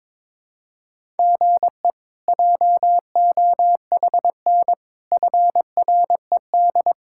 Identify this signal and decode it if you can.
Morse code, heard as a keyed tone, about 21 wpm, 700 Hz. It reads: GE JOHN FRED
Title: K1USN SST QSO Practice - 22wpm